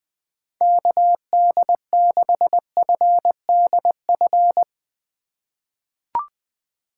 Morse code, heard as a keyed tone, about 20 wpm, 700 Hz. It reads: KD6FDF
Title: USA Call Signs - 6 Characters - 3x Character Spacing - 20wpm